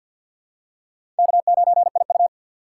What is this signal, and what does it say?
Morse code, heard as a keyed tone, about 51 wpm, 700 Hz. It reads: K9IU
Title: USA Call Signs - 4 Characters - 2x Character Spacing - 50wpm